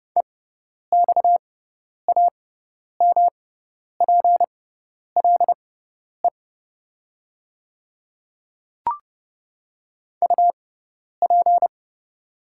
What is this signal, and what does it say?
Morse code, heard as a keyed tone, about 30 wpm, spaced as for 10 wpm, 700 Hz. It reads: EXAMPLE UP
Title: 6x Character Spacing - Top 300 Words - 30wpm